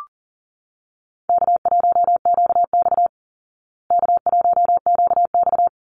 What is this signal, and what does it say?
Morse code, heard as a keyed tone, about 40 wpm, 700 Hz. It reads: K1QX K1QX
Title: Easy Call Signs Repeated 2x - 40wpm